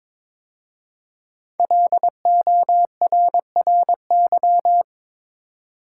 Morse code, heard as a keyed tone, about 22 wpm, 700 Hz. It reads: LORRY
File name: 5 Letter Words - First 3 Letters Given - 22wpm